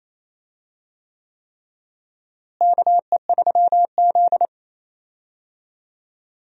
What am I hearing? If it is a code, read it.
Morse code, heard as a keyed tone, about 28 wpm, 700 Hz. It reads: KE3Z
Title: USA Call Signs - 4 Characters - Repeated 3x - 28wpm